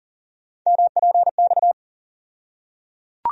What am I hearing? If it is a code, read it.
Morse code, heard as a keyed tone, about 40 wpm, 700 Hz. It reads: MPX
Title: USA Call Sign Postfixes - 3 Letters 40wpm